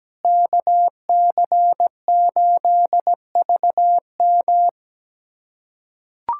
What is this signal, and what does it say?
Morse code, heard as a keyed tone, about 17 wpm, 700 Hz. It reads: KC8VM
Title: USA Callsign DE Callsign - 5 Characters - 17wpm